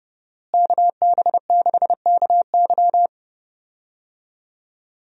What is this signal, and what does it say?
Morse code, heard as a keyed tone, about 30 wpm, 700 Hz. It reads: KB6KY
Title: USA Call Signs - 5 Characters - Repeated 3x - 30wpm